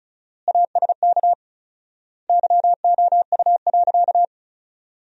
Morse code, heard as a keyed tone, about 36 wpm, 700 Hz. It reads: ASK YOU.
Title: 4x Word-Spacing - Sentences from Top 400 Words - 35wpm